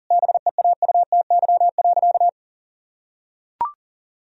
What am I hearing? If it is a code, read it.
Morse code, heard as a keyed tone, about 40 wpm, 700 Hz. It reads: BEAUTY.
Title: Sentences from Top 1000 Words - Review Part 2 - 40wpm